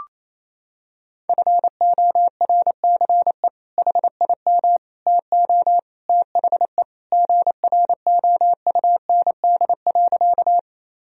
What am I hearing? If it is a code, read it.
Morse code, heard as a keyed tone, about 27 wpm, 700 Hz. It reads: FORCE HIM TO THE GROUND.
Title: Sentences from Top 1000 Words - Review Part 3 - 28wpm